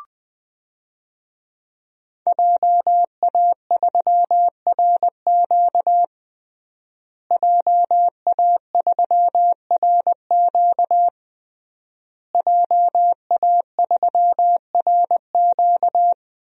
Morse code, heard as a keyed tone, about 21 wpm, 700 Hz. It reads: JA3RQ JA3RQ JA3RQ
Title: DXCC-100 Call Signs - 5 Characters - Repeated 3x - 20wpm